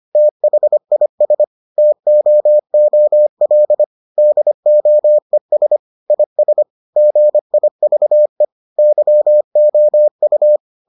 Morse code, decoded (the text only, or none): THIS TOOL DOES IS GIVE YOU